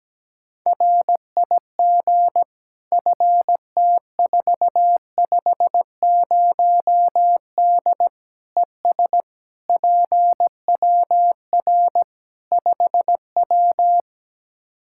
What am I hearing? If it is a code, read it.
Morse code, heard as a keyed tone, about 17 wpm, 700 Hz. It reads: RIG FT450D ES PWR 5W
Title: Practice QSO - Exchange 2 - Info - 17wpm